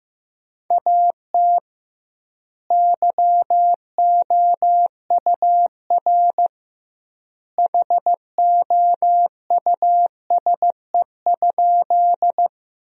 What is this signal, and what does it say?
Morse code, heard as a keyed tone, about 15 wpm, 700 Hz. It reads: AT YOUR HOUSE?
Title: 2x Word-Spacing - Sentences from Top 200 Words - 15wpm